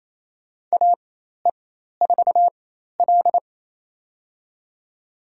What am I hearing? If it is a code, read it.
Morse code, heard as a keyed tone, about 28 wpm, spaced as for 13 wpm, 700 Hz. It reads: AE4L
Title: USA Call Signs - 4 Characters - 4x Character Spacing - 28wpm